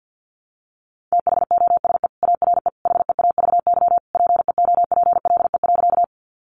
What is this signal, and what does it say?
Morse code, heard as a keyed tone, about 51 wpm, 700 Hz. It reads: THOSE ARE HEAVY PEOPLE.
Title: Sentences from Top 1100 Words 50wpm